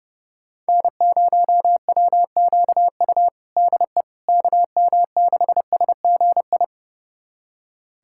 Morse code, heard as a keyed tone, about 32 wpm, 700 Hz. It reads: N0WQU DE KM6SGI
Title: USA Callsign DE Callsign 30wpm